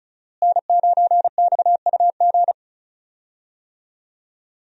Morse code, heard as a keyed tone, about 34 wpm, 700 Hz. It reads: N9XUG
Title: USA Call Signs - 5 Characters - Repeated 3x - 35wpm